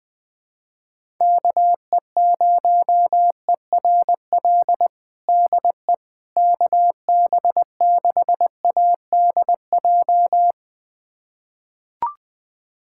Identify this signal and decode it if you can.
Morse code, heard as a keyed tone, about 20 wpm, 700 Hz. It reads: KE0ERL DE KB6ADJ
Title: USA Callsign DE Callsign 20wpm